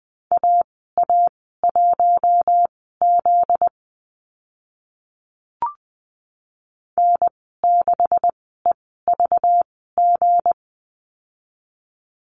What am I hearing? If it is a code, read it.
Morse code, heard as a keyed tone, about 20 wpm, spaced as for 15 wpm, 700 Hz. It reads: AA1Z N6EVG
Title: 245 Lesson 39g - Letter z - Call Signs 20wpm